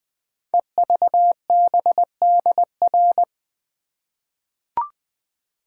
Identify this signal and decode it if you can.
Morse code, heard as a keyed tone, about 20 wpm, 700 Hz. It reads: EVBDR
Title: Code Groups - 5 Letter:Numbers - 20wpm